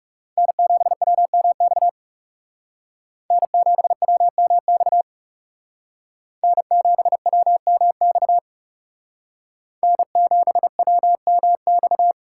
Morse code, heard as a keyed tone, about 39 wpm, 700 Hz. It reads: N7WMX N7WMX N7WMX N7WMX